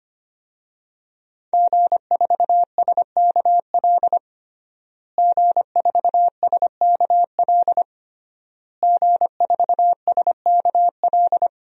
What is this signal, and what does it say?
Morse code, heard as a keyed tone, about 25 wpm, 700 Hz. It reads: G4SKL G4SKL G4SKL
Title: DXCC-100 Call Signs - 5 Characters - Repeated 3x - 25wpm